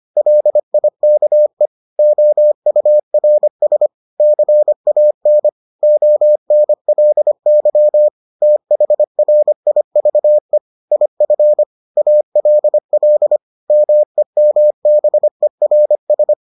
Morse code, heard as a keyed tone, about 25 wpm, 600 Hz. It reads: LIKE OURS CAN ONLY THRIVE IF ALL MEMBERS